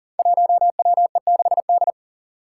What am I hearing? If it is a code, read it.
Morse code, heard as a keyed tone, about 40 wpm, 700 Hz. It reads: 1WE6D
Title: Five Challenging Letter-Number Combinations 40wpm